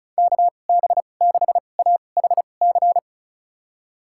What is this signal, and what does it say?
Morse code, heard as a keyed tone, about 36 wpm, spaced as for 25 wpm, 700 Hz. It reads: KB6AHC
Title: USA Call Signs - 6 Characters - 2x Character Spacing - 35wpm